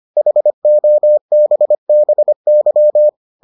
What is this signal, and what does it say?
Morse code, heard as a keyed tone, about 25 wpm, 600 Hz. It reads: HOBBY